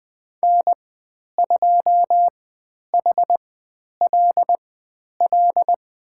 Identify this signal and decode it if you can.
Morse code, heard as a keyed tone, about 20 wpm, spaced as for 10 wpm, 700 Hz. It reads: N2HLL
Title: 109 Lesson 18g - Letter m - Call Signs 20wpm